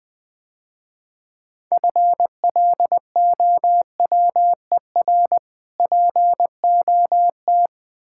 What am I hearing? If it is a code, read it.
Morse code, heard as a keyed tone, about 20 wpm, 700 Hz. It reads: FLOWER POT
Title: Easy Words - Speed-Racing with Hints - 20wpm